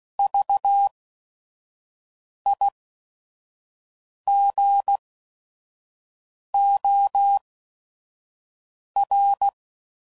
Morse code, heard as a keyed tone, about 16 wpm, spaced as for 5 wpm, 800 Hz. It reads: VIGOR
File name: Prefixes / By First Letter / Vs